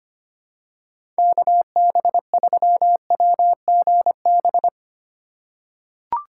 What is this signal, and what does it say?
Morse code, heard as a keyed tone, about 25 wpm, 700 Hz. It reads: KB3WGB
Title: USA Call Signs - 6 Characters - 4x Character Spacing - 25wpm